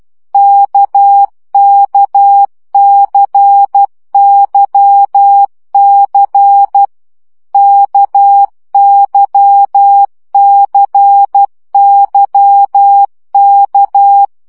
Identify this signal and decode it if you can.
Morse code, heard as a keyed tone, about 12 wpm, 800 Hz. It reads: KKCYC KYCYK
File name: CW gyakorlas k c y 10 baud